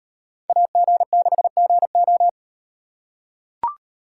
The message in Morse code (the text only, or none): AG6GO